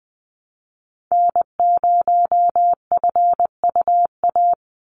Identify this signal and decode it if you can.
Morse code, heard as a keyed tone, about 20 wpm, 700 Hz. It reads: N0FUA
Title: USA Call Signs 20wpm